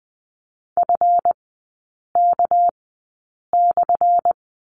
Morse code, heard as a keyed tone, about 20 wpm, spaced as for 8 wpm, 690 Hz. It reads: FK/
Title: 198 Lesson 32e - Letter k - Three Characters 20wpm